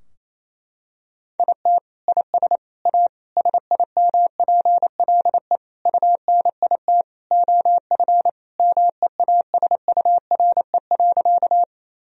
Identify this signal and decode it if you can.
Morse code, heard as a keyed tone, about 28 wpm, 700 Hz. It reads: IT IS A SIMPLE UNIT OF MEASURE.
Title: Sentences from Top 1000 Words - Review Part 2 - 28wpm